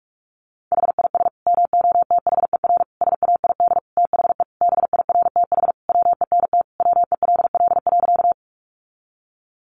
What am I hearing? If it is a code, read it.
Morse code, heard as a keyed tone, about 44 wpm, 700 Hz. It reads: HIS MOTHER SAID THE BIRTH WENT WELL.
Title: Sentences from Top 800 Words 45wpm